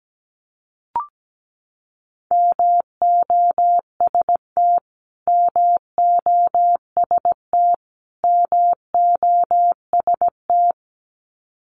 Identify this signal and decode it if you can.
Morse code, heard as a keyed tone, about 17 wpm, 700 Hz. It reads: MOST MOST MOST
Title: Top 500 Words Repeated 17wpm